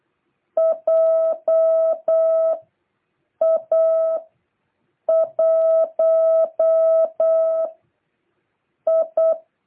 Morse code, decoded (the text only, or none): JA1I